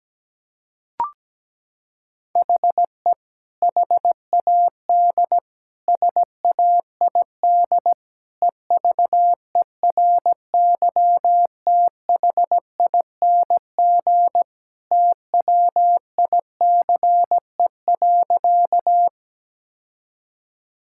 Morse code, heard as a keyed tone, about 17 wpm, 700 Hz. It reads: HE HAD SAID EVERYTHING TWICE.
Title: Sentences from Top 1600 Words 17wpm